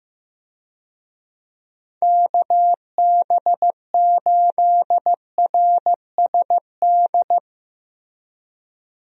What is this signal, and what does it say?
Morse code, heard as a keyed tone, about 15 wpm, 700 Hz. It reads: KB8RSD
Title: USA Call Signs - 6 Characters - 15wpm